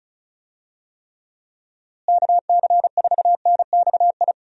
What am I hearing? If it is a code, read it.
Morse code, heard as a keyed tone, about 34 wpm, 700 Hz. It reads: KC4NXI
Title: USA Call Signs - 6 Characters - Repeated 2x - 35wpm